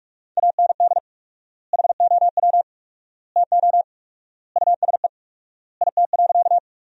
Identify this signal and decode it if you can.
Morse code, heard as a keyed tone, about 46 wpm, 690 Hz. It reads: AND HOW TO USE IT.
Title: 4x Word-Spacing - Sentences from Top 500 Words - 45wpm